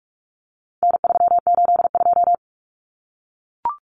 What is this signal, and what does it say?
Morse code, heard as a keyed tone, about 44 wpm, 700 Hz. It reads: N382
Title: Common Cut Numbers - 4-Digit 45wpm